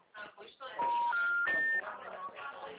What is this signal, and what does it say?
Special information tone, with speech.
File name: out- -2007